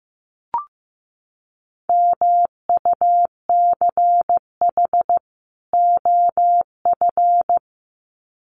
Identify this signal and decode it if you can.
Morse code, heard as a keyed tone, about 15 wpm, 700 Hz. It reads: MUCH OF
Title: Sets of 2 Words 15wpm